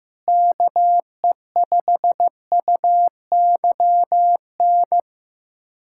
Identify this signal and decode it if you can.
Morse code, heard as a keyed tone, about 16 wpm, 700 Hz. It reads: KE5UYN
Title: USA Callsign DE Callsign 15wpm